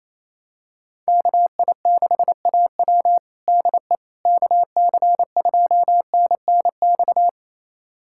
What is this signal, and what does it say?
Morse code, heard as a keyed tone, about 28 wpm, 700 Hz. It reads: KI6AW DE KC2NNX
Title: USA Callsign DE Callsign 28wpm